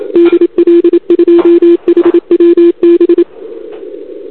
Morse code, heard as a keyed tone, about 27 wpm, 340 Hz. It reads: DL2HWB
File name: CQWWCW2021 / UA0AGI